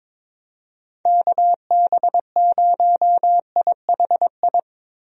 Morse code, heard as a keyed tone, about 22 wpm, 700 Hz. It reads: KB0IHI
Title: USA Call Signs - 6 Characters - 22wpm